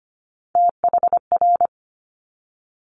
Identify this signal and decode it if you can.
Morse code, heard as a keyed tone, about 25 wpm, 700 Hz. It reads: THR
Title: Top Three Letter Combinations in Words 25wpm